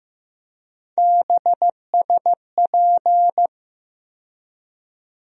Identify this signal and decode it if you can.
Morse code, heard as a keyed tone, about 16 wpm, 700 Hz. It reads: BSP
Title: USA Call Sign Postfixes - 3 Letters 15wpm